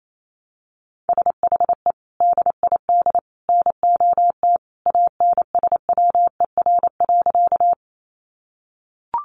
Morse code, heard as a keyed tone, about 28 wpm, 700 Hz. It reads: SHE DID NOT ANSWER.
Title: Sentences from Top 600 Words 28wpm